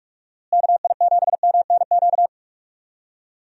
Morse code, heard as a keyed tone, about 46 wpm, 700 Hz. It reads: KI7MNQ
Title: USA Call Signs - 6 Characters - Repeated 2x - 45wpm